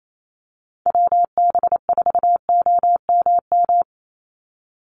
Morse code, heard as a keyed tone, about 28 wpm, 700 Hz. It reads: WB4OMM
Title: Call Signs Repeated 3x - 28wpm